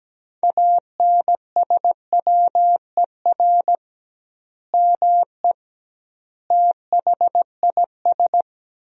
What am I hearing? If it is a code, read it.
Morse code, heard as a keyed tone, about 17 wpm, 700 Hz. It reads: ANSWER ME THIS
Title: 2x Word-Spacing - Sentences from Top 400 Words - 17wpm